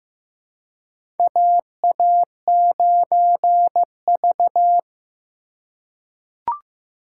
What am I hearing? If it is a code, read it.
Morse code, heard as a keyed tone, about 15 wpm, 700 Hz. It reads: AA9V
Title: USA Call Signs - 4 Characters - 3x Character Spacing - 15wpm